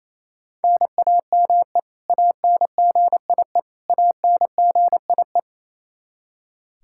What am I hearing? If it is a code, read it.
Morse code, heard as a keyed tone, about 28 wpm, 700 Hz. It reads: NAME ANGIE ANGIE
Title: Practice QSO - Exchange 1 - Beginning - 28wpm